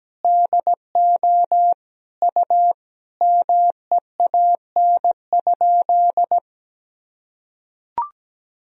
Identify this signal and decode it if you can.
Morse code, heard as a keyed tone, about 17 wpm, 700 Hz. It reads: DO U MEAN?
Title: Sentences from Top 500 Words and Abbreviations 17wpm